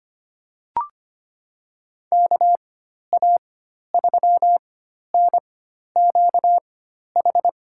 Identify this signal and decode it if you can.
Morse code, heard as a keyed tone, about 25 wpm, spaced as for 12 wpm, 700 Hz. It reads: KA3NQH
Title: USA Call Signs - 6 Characters - 4x Character Spacing - 25wpm